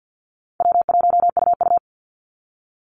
Morse code, heard as a keyed tone, about 50 wpm, 700 Hz. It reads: W1UU
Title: Easy Call Signs - 50wpm